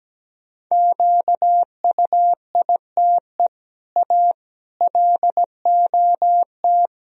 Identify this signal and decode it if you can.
Morse code, heard as a keyed tone, about 17 wpm, 700 Hz. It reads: QUITE A LOT
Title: Sentences from Top 1000 Words - Review Part 3 - 17wpm